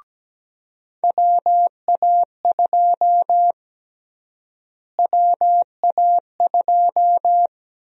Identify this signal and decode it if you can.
Morse code, heard as a keyed tone, about 17 wpm, 700 Hz. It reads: WA2 WA2
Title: USA Call Sign Prefixes - Repeated 2x - 17wpm